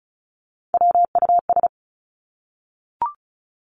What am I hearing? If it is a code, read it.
Morse code, heard as a keyed tone, about 36 wpm, 700 Hz. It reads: WUS
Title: Three Challenging Letter-Number Combinations 35wpm